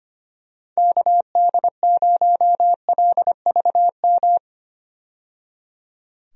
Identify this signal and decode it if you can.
Morse code, heard as a keyed tone, about 25 wpm, 700 Hz. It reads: KD0LVM
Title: USA Call Signs - 6 Characters - 25wpm